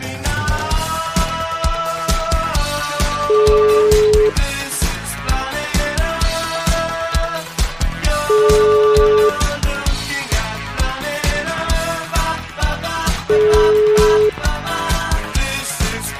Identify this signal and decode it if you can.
Ringback tone (Europe, 425 Hz), with noise.